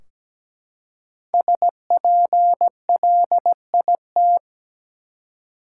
Morse code, heard as a keyed tone, about 17 wpm, 700 Hz. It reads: SPLIT
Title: Five Letter Words - 17wpm